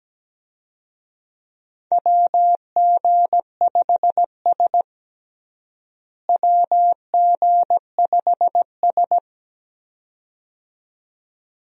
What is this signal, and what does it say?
Morse code, heard as a keyed tone, about 17 wpm, 700 Hz. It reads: WG5S WG5S
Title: USA Call Signs - 4 Characters - Repeated 2x - 17wpm